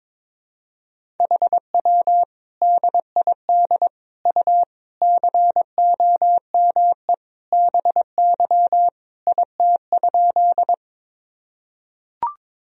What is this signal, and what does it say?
Morse code, heard as a keyed tone, about 22 wpm, 700 Hz. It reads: HW DID U COME BY IT?
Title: Sentences from Top 100 Words and Abbreviations 22wpm